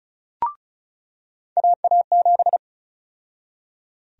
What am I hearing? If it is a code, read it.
Morse code, heard as a keyed tone, about 34 wpm, 700 Hz. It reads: AA7
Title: USA Call Sign Prefixes - 35wpm